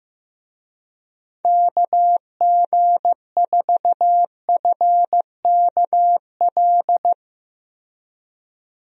Morse code, heard as a keyed tone, about 15 wpm, 700 Hz. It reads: KG4FKL
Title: USA Call Signs - 6 Characters - Repeated 2x - 15wpm